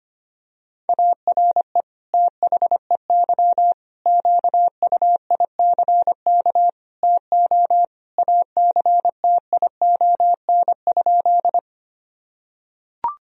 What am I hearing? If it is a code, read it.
Morse code, heard as a keyed tone, about 25 wpm, 700 Hz. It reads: ARE THEY QUICK TO ACTION?
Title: Sentences from Top 1000 Words - Review Part 4 - 25wpm